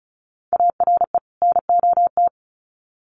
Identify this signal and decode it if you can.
Morse code, heard as a keyed tone, about 34 wpm, 700 Hz. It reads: ARE NOT
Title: Sets of 2 Words 35wpm